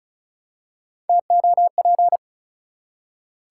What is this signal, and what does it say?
Morse code, heard as a keyed tone, about 34 wpm, 700 Hz. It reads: TOP
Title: Three Letter Words - 35wpm